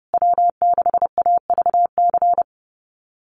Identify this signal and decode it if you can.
Morse code, heard as a keyed tone, about 30 wpm, 700 Hz. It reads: W6AVC
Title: USA Call Signs Repeated 3x 30wpm